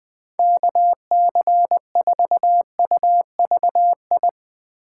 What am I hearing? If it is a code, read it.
Morse code, heard as a keyed tone, about 22 wpm, 700 Hz. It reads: KC4UVI
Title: Speed Racing - USA Call Signs 20wpm